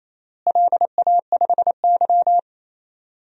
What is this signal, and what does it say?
Morse code, heard as a keyed tone, about 28 wpm, 700 Hz. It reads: LA5Y
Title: DXCC-100 Call Signs - 4 Characters - Repeated 2x - 28wpm